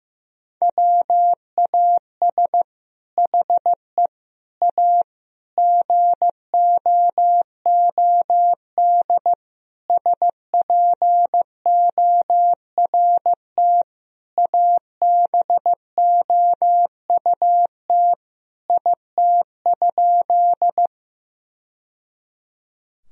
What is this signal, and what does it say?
Morse code, heard as a keyed tone, about 16 wpm, 700 Hz. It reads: WAS HE A GOOD SPORT ABOUT IT?
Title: Sentences from Top 1000 Words - Review Part 4 - 15wpm